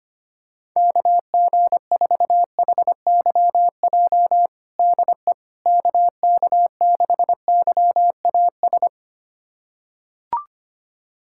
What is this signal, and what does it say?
Morse code, heard as a keyed tone, about 25 wpm, 700 Hz. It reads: KG4HYJ DE KK6YAS
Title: USA Callsign DE Callsign - 6 Characters - 25wpm